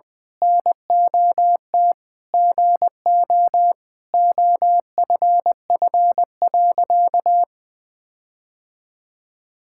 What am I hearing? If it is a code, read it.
Morse code, heard as a keyed tone, about 20 wpm, 700 Hz. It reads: NOT GO OFF.
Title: Sentences with 4-Word Phrases - Spoken Beginning - 20wpm